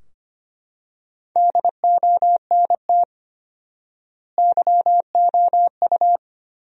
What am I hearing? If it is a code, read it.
Morse code, heard as a keyed tone, about 25 wpm, 700 Hz. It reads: DONT YOU
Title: 4x Word-Spacing - Sentences from Top 300 Words - 25wpm